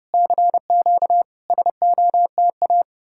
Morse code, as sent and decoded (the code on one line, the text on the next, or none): -.-. --.- / ... --- - .-
CQ SOTA